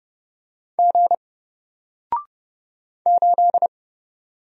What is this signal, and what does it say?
Morse code, heard as a keyed tone, about 30 wpm, 700 Hz. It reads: G 8
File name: Single-Letter-Number-30wpm